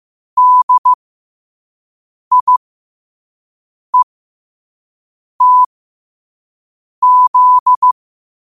Morse code, heard as a keyed tone, about 15 wpm, spaced as for 5 wpm, 1000 Hz. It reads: DIETZ
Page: Quotes for Tue, 12 Aug 2025 in Morse Code at 5 words per minute.